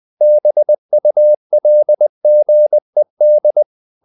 Morse code, heard as a keyed tone, about 20 wpm, 600 Hz. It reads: BULGED
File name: the old man and the sea / the old man and the sea part 1 20wpm